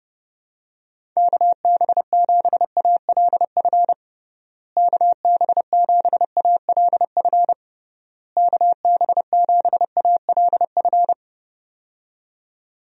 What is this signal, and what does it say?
Morse code, heard as a keyed tone, about 32 wpm, 700 Hz. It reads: KB7ALF KB7ALF KB7ALF
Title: USA Call Signs - 6 Characters - Repeated 3x - 30wpm